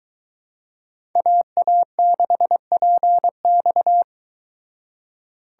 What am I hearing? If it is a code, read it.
Morse code, heard as a keyed tone, about 23 wpm, 700 Hz. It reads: AA6PX